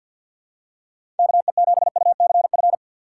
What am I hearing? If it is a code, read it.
Morse code, heard as a keyed tone, about 53 wpm, 690 Hz. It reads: KE7UCF
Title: USA Call Signs - 6x Character Spacing - 50wpm